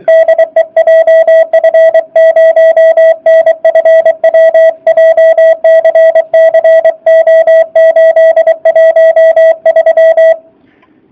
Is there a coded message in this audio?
Yes: Morse code from 0.1 to 10.3 s, heard as a keyed tone, about 24 wpm, 660 Hz. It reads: DE JF0NFW JCC O813